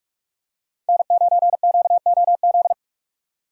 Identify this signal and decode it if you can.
Morse code, heard as a keyed tone, about 41 wpm, 700 Hz. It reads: N9QOZ